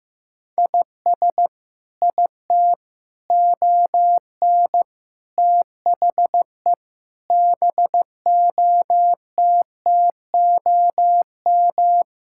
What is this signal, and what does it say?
Morse code, heard as a keyed tone, about 14 wpm, 700 Hz. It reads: IS IT ON THE BOTTOM
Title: Sentences from Top 1000 Words - Review Part 6 - 15wpm